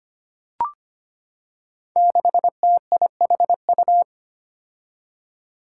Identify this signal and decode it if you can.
Morse code, heard as a keyed tone, about 25 wpm, 700 Hz. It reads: 6TIHU
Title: Five Challenging Letter-Number Combinations 25wpm